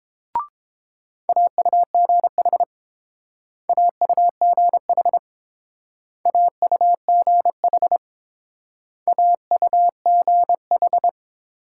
Morse code, heard as a keyed tone, about 28 wpm, 700 Hz. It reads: AUGH AUGH AUGH AUGH